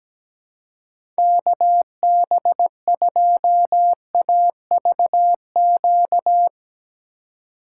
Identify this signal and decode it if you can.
Morse code, heard as a keyed tone, about 17 wpm, 700 Hz. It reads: KB2AVQ
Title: USA Call Signs - 6 Characters - 4x Character Spacing - 17wpm